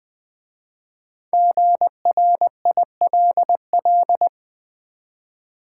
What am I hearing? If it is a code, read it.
Morse code, heard as a keyed tone, about 20 wpm, 700 Hz. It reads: GRILL
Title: Easy Words with Hints - 20wpm